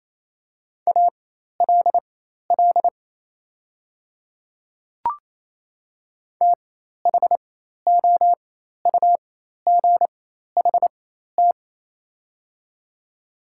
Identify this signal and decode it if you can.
Morse code, heard as a keyed tone, about 27 wpm, spaced as for 13 wpm, 690 Hz. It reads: ALL THOUGHT
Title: 4x Character Spacing - Top 300 Words - 28wpm